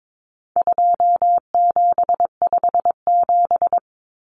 Morse code, heard as a keyed tone, about 22 wpm, 700 Hz. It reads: 2757
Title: Common Cut Numbers - 4-Digit 22wpm